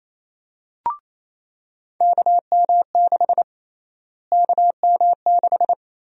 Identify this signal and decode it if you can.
Morse code, heard as a keyed tone, about 27 wpm, 700 Hz. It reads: KM6 KM6
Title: USA Call Sign Prefixes - Repeated 2x - 28wpm